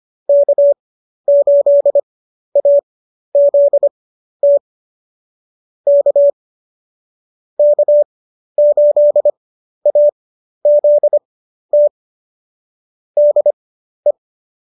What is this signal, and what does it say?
Morse code, heard as a keyed tone, about 25 wpm, spaced as for 12 wpm, 590 Hz. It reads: K8AZT K K8AZT DE